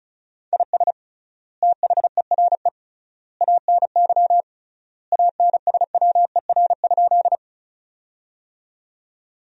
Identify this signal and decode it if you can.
Morse code, heard as a keyed tone, about 36 wpm, 700 Hz. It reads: IS THERE ANY ANSWER?
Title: 3x Word-Spacing - Sentences from Top 300 Words - 35wpm